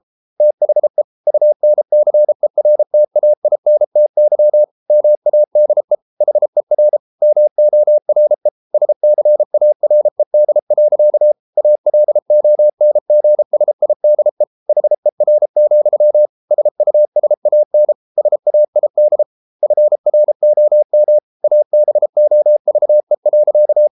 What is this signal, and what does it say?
Morse code, heard as a keyed tone, about 33 wpm, 600 Hz. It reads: THE UNCERTAINTY MADE HER MORE SCARED. ALONGSIDE HER, SUSAN SAID FROM ABOVE.